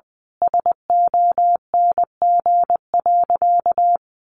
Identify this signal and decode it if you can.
Morse code, heard as a keyed tone, about 19 wpm, 700 Hz. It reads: SONG.
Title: Sentences from Top 900 Words 20wpm